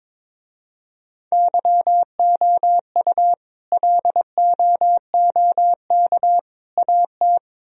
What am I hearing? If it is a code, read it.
Morse code, heard as a keyed tone, about 24 wpm, 700 Hz. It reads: YOU LOOK AT
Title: Speed-Racing - Sets of 3 Words 22wpm